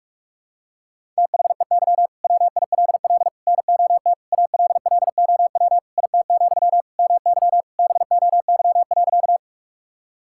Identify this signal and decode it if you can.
Morse code, heard as a keyed tone, about 41 wpm, 700 Hz. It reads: THEY WILL NOT ALLOW IT, MY BOY.
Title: Sentences from Top 1000 Words - Review Part 5 - 45wpm